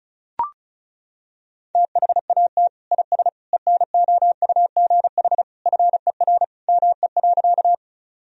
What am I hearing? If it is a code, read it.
Morse code, heard as a keyed tone, about 34 wpm, 700 Hz. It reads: THAT IS ENOUGH FER ME.
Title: Sentences from Top 400 Words and Abbreviations 35wpm